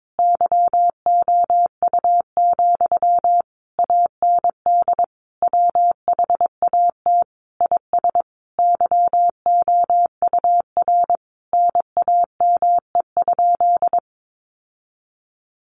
Morse code, heard as a keyed tone, about 22 wpm, 700 Hz. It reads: YOU, AND WHAT IS YOUR NAME?
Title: Sentences from Top 1200 Words 22wpm